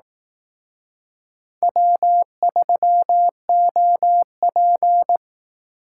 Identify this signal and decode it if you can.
Morse code, heard as a keyed tone, about 20 wpm, 700 Hz. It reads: W3OP